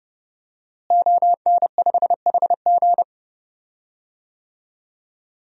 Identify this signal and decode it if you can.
Morse code, heard as a keyed tone, about 30 wpm, 700 Hz. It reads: ON5HG
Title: DXCC-100 Call Signs - 5 Characters - Repeated 3x - 30wpm